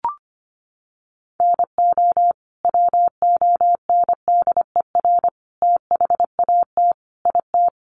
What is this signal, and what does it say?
Morse code, heard as a keyed tone, about 25 wpm, 700 Hz. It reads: NO WONDER THAT IT
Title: Sentences from Top 1100 Words 25wpm